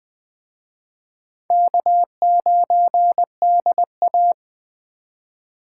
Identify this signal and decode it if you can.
Morse code, heard as a keyed tone, about 20 wpm, 700 Hz. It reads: K9DA